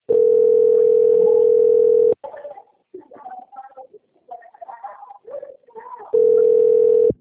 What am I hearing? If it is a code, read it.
Ringback tone (North America), 440 + 480 Hz.